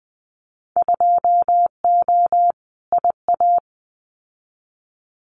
Morse code, heard as a keyed tone, about 19 wpm, 700 Hz. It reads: 2O IA
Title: Winter Field Day Exchanges 20wpm